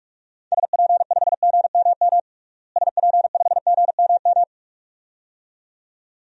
Morse code, heard as a keyed tone, about 46 wpm, 700 Hz. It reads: SP5GMM SP5GMM
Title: DXCC-100 Call Signs - 6 Characters - Repeated 2x - 45wpm